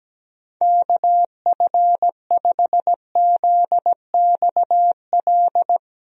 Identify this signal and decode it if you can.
Morse code, heard as a keyed tone, about 17 wpm, 700 Hz. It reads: KF5ZXL
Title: USA Call Signs - 6 Characters - 17wpm